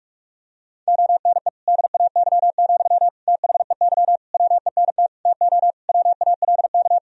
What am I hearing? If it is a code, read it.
Morse code, heard as a keyed tone, about 46 wpm, 700 Hz. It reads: ONE DAY, THEY WENT TO WALK
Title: Sentences from Top 1000 Words - Review Part 2 - 45wpm